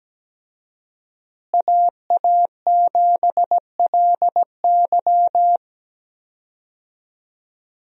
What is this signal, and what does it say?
Morse code, heard as a keyed tone, about 17 wpm, 700 Hz. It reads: AA7LY
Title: USA Call Signs - 5 Characters - Repeated 3x - 17wpm